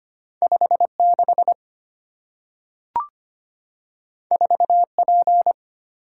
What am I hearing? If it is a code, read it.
Morse code, heard as a keyed tone, about 25 wpm, 690 Hz. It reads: 56 4P
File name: Two Letter-Number Combinations - 25wpm